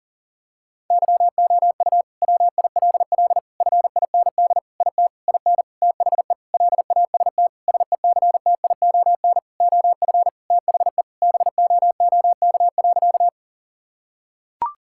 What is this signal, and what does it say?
Morse code, heard as a keyed tone, about 40 wpm, 700 Hz. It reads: YOU WILL FIND IT IN THE LAST SECTION OF THE BOOK.
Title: Sentences from Top 1000 Words - Review Part 5 - 40wpm